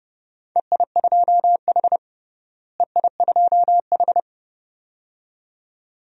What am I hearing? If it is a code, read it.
Morse code, heard as a keyed tone, about 32 wpm, 700 Hz. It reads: EI2H EI2H
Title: DXCC-100 Call Signs - 4 Characters - Repeated 2x - 30wpm